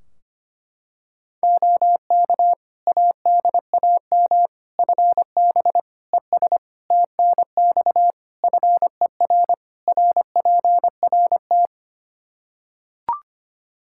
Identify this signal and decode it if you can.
Morse code, heard as a keyed tone, about 25 wpm, 700 Hz. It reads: OK ADAM FB ES TNX FER RPRT
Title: Practice QSO - Exchange 2 - Info - 25wpm